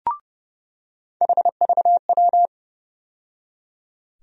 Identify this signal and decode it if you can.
Morse code, heard as a keyed tone, about 30 wpm, 700 Hz. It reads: HVW
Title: USA Call Sign Postfixes - 3 Letters 30wpm